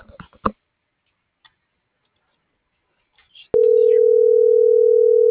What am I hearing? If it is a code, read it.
Ringback tone (North America), 440 + 480 Hz.
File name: CENTRAL SOTOVA / out alejandro.rodriguez - .54390